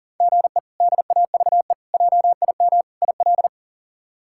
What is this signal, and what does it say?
Morse code, heard as a keyed tone, about 40 wpm, 700 Hz. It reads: GE DAVE JIM IL
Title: K1USN SST QSO Practice - 40wpm